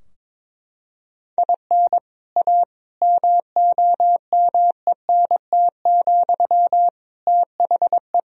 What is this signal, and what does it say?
Morse code, heard as a keyed tone, about 22 wpm, 700 Hz. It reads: IN A MOMENT, THE
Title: Sentences from Top 1000 Words - Review Part 3 - 22wpm